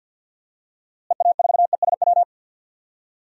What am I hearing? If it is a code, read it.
Morse code, heard as a keyed tone, about 52 wpm, 700 Hz. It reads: EA4ESW
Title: DXCC-100 Call Signs - 6 Characters - Repeated 2x - 50wpm